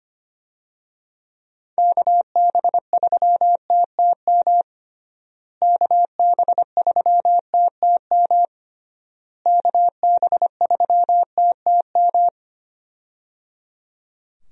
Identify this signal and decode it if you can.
Morse code, heard as a keyed tone, about 25 wpm, 700 Hz. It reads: KB3TTM KB3TTM KB3TTM
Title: USA Call Signs - 6 Characters - Repeated 3x - 25wpm